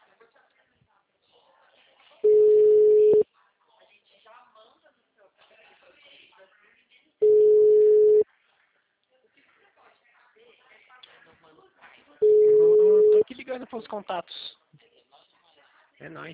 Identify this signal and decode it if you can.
Ringback tone (Europe, 425 Hz), with speech.
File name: out- -2011- - - .36347